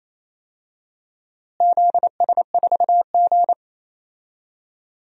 Morse code, heard as a keyed tone, about 28 wpm, 700 Hz. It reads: ZS4G
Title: DXCC-100 Call Signs - 4 Characters - Repeated 2x - 28wpm